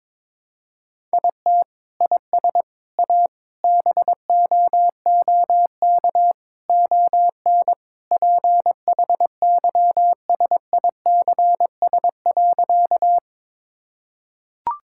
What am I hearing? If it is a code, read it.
Morse code, heard as a keyed tone, about 21 wpm, 700 Hz. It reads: IT IS A BOOK ON PHYSICS.
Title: Sentences from Top 1000 Words - Review Part 5 - 22wpm